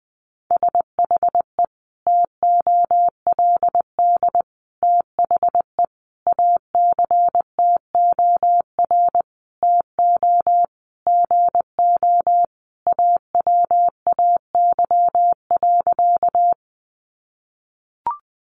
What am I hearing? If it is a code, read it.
Morse code, heard as a keyed tone, about 20 wpm, 700 Hz. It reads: SHE TOLD THE ACTOR TO GO AWAY.
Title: Sentences from Top 1500 Words 20wpm